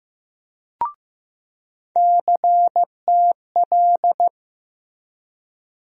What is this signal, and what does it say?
Morse code, heard as a keyed tone, about 15 wpm, 700 Hz. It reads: CTL
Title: USA Call Sign Postfixes - 3 Letters 15wpm